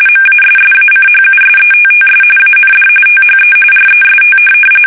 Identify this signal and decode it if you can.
RTTY, about 45.45 baud. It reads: MASSACHUSETTS AND THE REST
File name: fsk850 / USS-Massachusetts